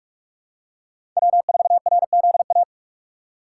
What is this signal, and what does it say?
Morse code, heard as a keyed tone, about 46 wpm, 700 Hz. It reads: W4RZA
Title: USA Call Signs - 2x Character Spacing - 45wpm